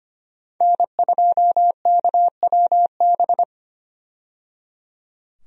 Morse code, heard as a keyed tone, about 25 wpm, 700 Hz. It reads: N2KWB
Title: USA Call Signs - 5 Characters - 25wpm